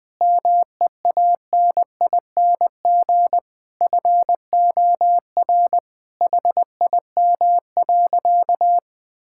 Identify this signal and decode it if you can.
Morse code, heard as a keyed tone, about 20 wpm, 700 Hz. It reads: MEANING FOR HIM.
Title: Sentences from Top 1000 Words - Review Part 6 - 20wpm